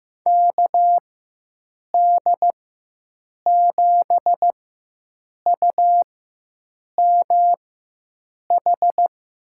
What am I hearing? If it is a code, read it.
Morse code, heard as a keyed tone, about 15 wpm, spaced as for 7 wpm, 700 Hz. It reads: KD7UMH
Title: USA Call Signs - 6 Characters - 4x Character Spacing - 15wpm